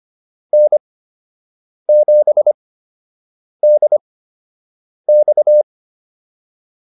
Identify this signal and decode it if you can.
Morse code, heard as a keyed tone, about 25 wpm, spaced as for 7 wpm, 600 Hz. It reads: N7DX